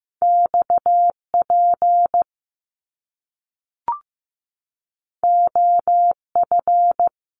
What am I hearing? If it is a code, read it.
Morse code, heard as a keyed tone, about 15 wpm, 700 Hz. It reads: XP OF
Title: Top Two Letter Combinations in Words 15wpm